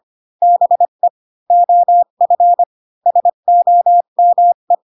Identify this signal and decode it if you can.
Morse code, heard as a keyed tone, about 25 wpm, 700 Hz. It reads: BE OF SOME
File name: Pinocchio / Pinocchio ch7 20 25 ik1hge